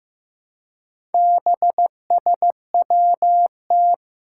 Morse code, heard as a keyed tone, about 15 wpm, 700 Hz. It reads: BSWT
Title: Four Challenging Letter-Number Combinations 15wpm